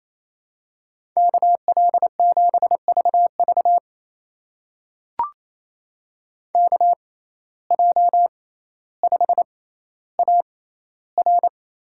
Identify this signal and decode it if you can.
Morse code, heard as a keyed tone, about 27 wpm, spaced as for 14 wpm, 700 Hz. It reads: KL7VV KJ5AR
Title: USA Call Signs - 5 Characters - 6x Character Spacing - 28wpm